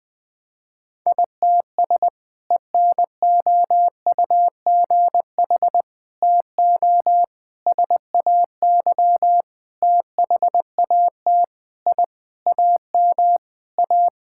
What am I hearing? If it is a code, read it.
Morse code, heard as a keyed tone, about 20 wpm, 700 Hz. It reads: ITS ENOUGH TO SAY THAT I AM A
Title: Sentences from Top 1000 Words - Review Part 6 - 20wpm